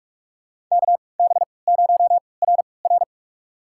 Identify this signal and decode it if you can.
Morse code, heard as a keyed tone, about 46 wpm, spaced as for 25 wpm, 700 Hz. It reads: KB0RR
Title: USA Call Signs - 5 Characters - 3x Character Spacing - 45wpm